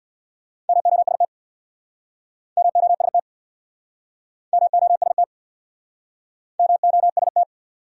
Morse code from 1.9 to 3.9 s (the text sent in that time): MOST